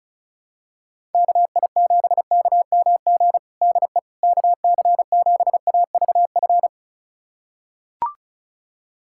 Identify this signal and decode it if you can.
Morse code, heard as a keyed tone, about 36 wpm, 700 Hz. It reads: KI7KMG DE KC7AVF
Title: USA Callsign DE Callsign - 6 Characters - 35wpm